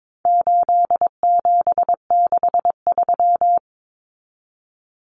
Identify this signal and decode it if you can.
Morse code, heard as a keyed tone, about 22 wpm, 700 Hz. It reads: 8763
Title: Common Cut Numbers - 4-Digit 22wpm